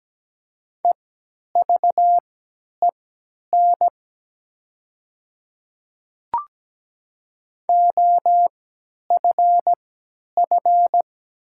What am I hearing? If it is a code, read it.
Morse code, heard as a keyed tone, about 17 wpm, spaced as for 10 wpm, 700 Hz. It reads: EVEN OFF
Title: 3x Character Spacing - Top 200 Words - 17wpm